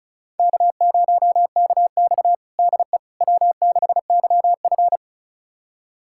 Morse code, heard as a keyed tone, about 36 wpm, 700 Hz. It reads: K0KX DE W6YF
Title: USA Callsign DE Callsign - 4 Characters - 35wpm